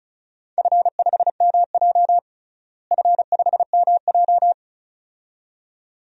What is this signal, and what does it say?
Morse code, heard as a keyed tone, about 34 wpm, 700 Hz. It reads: F5MJ F5MJ
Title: DXCC-100 Call Signs - 4 Characters - Repeated 2x - 35wpm